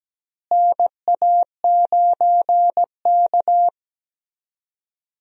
Morse code, heard as a keyed tone, about 17 wpm, 700 Hz. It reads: NA9K
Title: USA Call Signs - 4 Characters - Repeated 2x - 17wpm